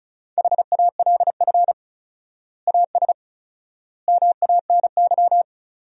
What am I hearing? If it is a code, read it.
Morse code, heard as a keyed tone, about 36 wpm, 700 Hz. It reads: HALF AS MANY
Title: 4x Word-Spacing - Sentences from Top 400 Words - 35wpm